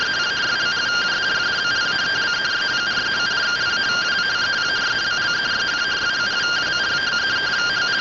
Two-tone teleprinter signal, about 170 Hz shift.
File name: findings / CT SAGITAL 27.gen-12 rtty rx